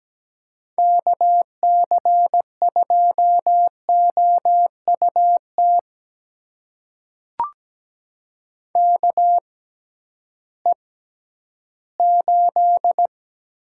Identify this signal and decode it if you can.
Morse code, heard as a keyed tone, about 17 wpm, spaced as for 11 wpm, 700 Hz. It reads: KC2OUT KE8
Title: USA Call Signs - 6 Characters - 6x Character Spacing - 17wpm